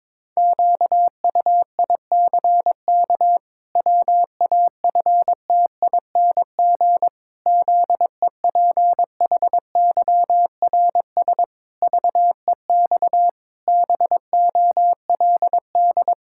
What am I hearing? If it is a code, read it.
Morse code, heard as a keyed tone, about 22 wpm, 700 Hz. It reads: QUICK WAFTING ZEPHYRS VEX BOLD
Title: Pangram Challenge v2 - 22wpm